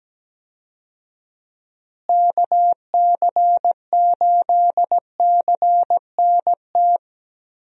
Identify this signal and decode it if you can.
Morse code, heard as a keyed tone, about 17 wpm, 700 Hz. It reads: KC8CNT
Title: USA Call Signs - 6 Characters - Repeated 2x - 17wpm